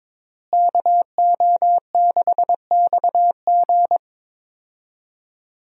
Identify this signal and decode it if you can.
Morse code, heard as a keyed tone, about 22 wpm, 700 Hz. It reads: KO6XG
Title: USA Call Signs - 5 Characters - Repeated 3x - 22wpm